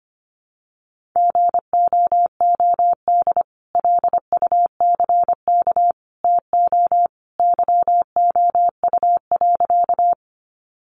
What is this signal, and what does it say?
Morse code, heard as a keyed tone, about 25 wpm, 700 Hz. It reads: GOOD LUCK TO YOU.
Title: Sentences from Top 1500 Words 25wpm